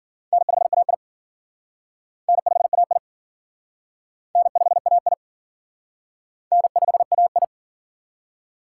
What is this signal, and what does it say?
Morse code, heard as a keyed tone, about 50 wpm, 700 Hz. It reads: N5AI N5AI N5AI N5AI
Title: Speed Racing - USA 1x2 Call Signs 40wpm